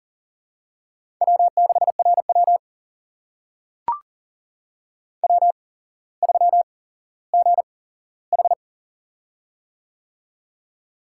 Morse code, heard as a keyed tone, about 38 wpm, spaced as for 17 wpm, 700 Hz. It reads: W6RW W3GH
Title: USA Call Signs - 4 Characters - 8x Character Spacing - 40wpm